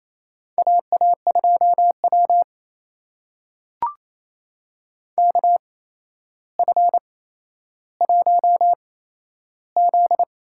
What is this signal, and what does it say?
Morse code, heard as a keyed tone, about 28 wpm, spaced as for 12 wpm, 700 Hz. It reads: AA2W KF1Z
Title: USA Call Signs - 4 Characters - 8x Character Spacing - 28wpm